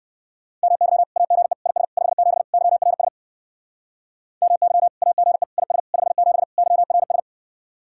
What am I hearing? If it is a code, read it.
Morse code, heard as a keyed tone, about 57 wpm, 700 Hz. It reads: MY AGE IS 57 YRS MY AGE IS 57 YRS